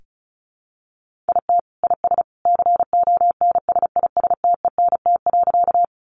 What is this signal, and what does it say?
Morse code, heard as a keyed tone, about 34 wpm, 700 Hz. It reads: IT IS CONSISTENT.
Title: Sentences from Top 1500 Words 35wpm